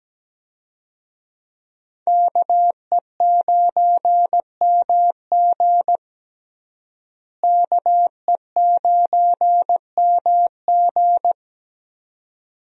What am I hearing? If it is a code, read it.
Morse code, heard as a keyed tone, about 17 wpm, 700 Hz. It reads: KE9MG KE9MG
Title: USA Call Signs - 5 Characters - Repeated 3x - 17wpm